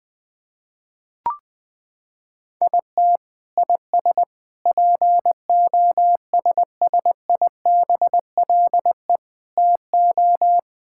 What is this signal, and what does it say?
Morse code, heard as a keyed tone, about 20 wpm, 700 Hz. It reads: IT IS POSSIBLE TO
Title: Sentences from Top 1000 Words - Review Part 6 - 20wpm